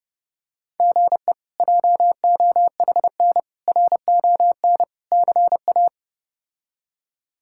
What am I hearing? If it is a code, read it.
Morse code, heard as a keyed tone, about 30 wpm, 700 Hz. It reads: GE JOHN RON CA
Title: K1USN SST QSO Practice - 30wpm